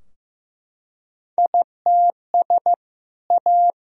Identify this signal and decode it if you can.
Morse code, heard as a keyed tone, about 15 wpm, 700 Hz. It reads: ITS A
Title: Sentences from Top 1000 Words - Review Part 4 - 15wpm